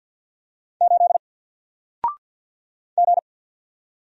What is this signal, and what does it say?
Morse code, heard as a keyed tone, about 50 wpm, 700 Hz. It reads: 8 G
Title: Single Letter-Number 50wpm